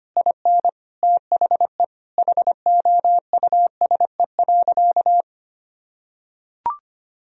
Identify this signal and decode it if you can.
Morse code, heard as a keyed tone, about 25 wpm, 690 Hz. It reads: IN THE HOUSE.
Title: Sentences from Top 1000 Words - Review Part 4 - 25wpm